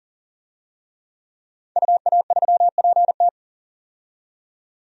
Morse code, heard as a keyed tone, about 40 wpm, 700 Hz. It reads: UA3PT
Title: DXCC-100 Call Signs - 5 Characters - 40wpm